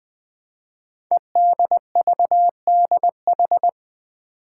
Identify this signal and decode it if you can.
Morse code, heard as a keyed tone, about 20 wpm, 700 Hz. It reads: EDVDH
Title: Code Groups - 5 Letters - 20wpm